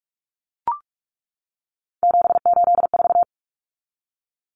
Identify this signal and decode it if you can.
Morse code, heard as a keyed tone, about 44 wpm, 700 Hz. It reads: 784
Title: Numbers - 3-Digit 45wpm